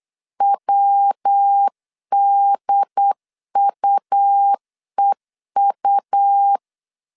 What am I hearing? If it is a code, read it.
Morse code, heard as a keyed tone, about 8 wpm, 800 Hz. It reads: WDUEU
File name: CW / DET / 1 3 1500C S A 8wpm 4229